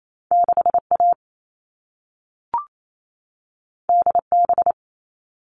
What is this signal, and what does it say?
Morse code, heard as a keyed tone, about 28 wpm, 700 Hz. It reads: 6A DB
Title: All Cut Numbers - 2-Digit 28wpm